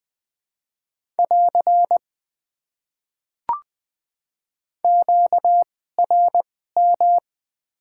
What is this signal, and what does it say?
Morse code, heard as a keyed tone, about 20 wpm, spaced as for 14 wpm, 700 Hz. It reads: <AR> QRM
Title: Top 90 QSO Elements - 2x Character Spacing - 20wpm